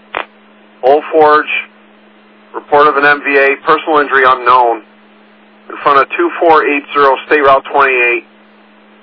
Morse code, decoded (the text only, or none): none